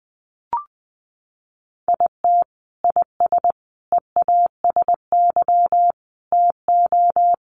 Morse code, heard as a keyed tone, about 20 wpm, 700 Hz. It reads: IT IS EASY TO
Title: Sentences from Top 1300 Words 20wpm